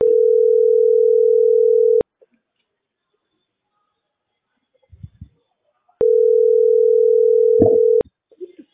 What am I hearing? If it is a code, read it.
Ringback tone (North America), 440 + 480 Hz.